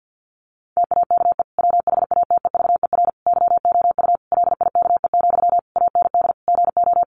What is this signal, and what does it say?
Morse code, heard as a keyed tone, about 50 wpm, 700 Hz. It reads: TAKE WHATEVER YOU LIKE, AND GO
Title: Sentences from Top 1100 Words 50wpm